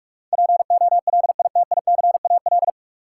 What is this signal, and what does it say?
Morse code, heard as a keyed tone, about 46 wpm, 700 Hz. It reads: POLITICAL
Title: Sentences from Top 800 Words - New Word Drill - 45wpm